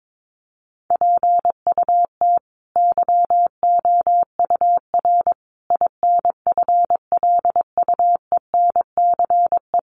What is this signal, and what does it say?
Morse code, heard as a keyed tone, about 22 wpm, 700 Hz. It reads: PUT YOUR INFLUENCE
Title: Sentences from Top 1100 Words 22wpm